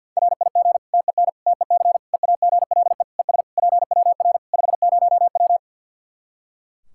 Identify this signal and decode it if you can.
Morse code, heard as a keyed tone, about 48 wpm, 700 Hz. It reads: RIG TEN TEC EAGLE ES PWR 50W
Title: Practice QSO - Exchange 2 - Info - 50wpm